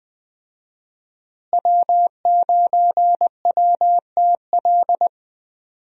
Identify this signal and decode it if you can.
Morse code, heard as a keyed tone, about 20 wpm, 700 Hz. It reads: W9WTL
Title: USA Call Signs - 5 Characters - Repeated 2x - 20wpm